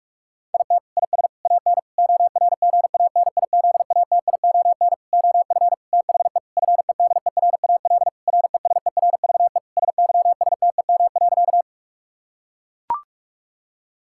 Morse code, heard as a keyed tone, about 41 wpm, 700 Hz. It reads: IT IS AN ORGANIZATION OF THE FEDERAL RESERVE SYSTEM.
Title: Sentences from Top 1000 Words - Review Part 6 - 45wpm